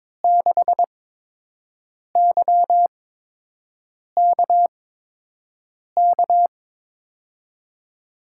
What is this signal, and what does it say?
Morse code, heard as a keyed tone, about 22 wpm, spaced as for 6 wpm, 700 Hz. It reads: 6YKK
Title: USA Call Signs - 6 Characters - 8x Character Spacing - 22wpm